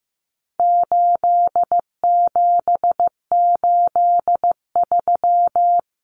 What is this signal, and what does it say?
Morse code, heard as a keyed tone, about 15 wpm, 700 Hz. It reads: 8783
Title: Numbers - 4-Digit 15wpm